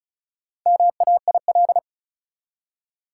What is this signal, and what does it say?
Morse code, heard as a keyed tone, about 34 wpm, 700 Hz. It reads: MAIL
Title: Four Letter Words - 35wpm